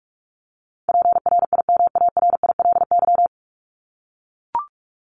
Morse code, heard as a keyed tone, about 44 wpm, 700 Hz. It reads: PRIMARILY
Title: Top 1001-1250 Words 45wpm